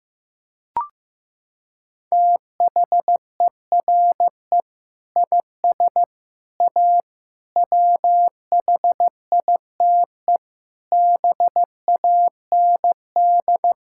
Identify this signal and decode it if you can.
Morse code, heard as a keyed tone, about 15 wpm, 700 Hz. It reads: THERE IS A WHITE BAND
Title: Sentences from Top 1600 Words 15wpm